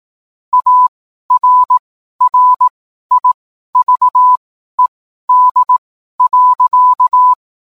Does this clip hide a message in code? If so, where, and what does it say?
Yes: Morse code from 0.5 to 7.3 s, heard as a keyed tone, about 18 wpm, spaced as for 12 wpm, 1000 Hz. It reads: ARRIVED.